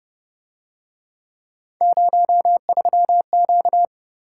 Morse code, heard as a keyed tone, about 30 wpm, 700 Hz. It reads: 03Q
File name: Speed-Racing - Three Letter-Number Combinations - 25wpm